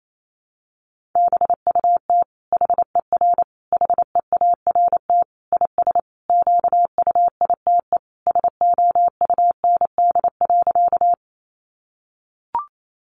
Morse code, heard as a keyed tone, about 28 wpm, 700 Hz. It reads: BUT HER HEART IS QUITE SOUND.
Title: Sentences from Top 1200 Words 28wpm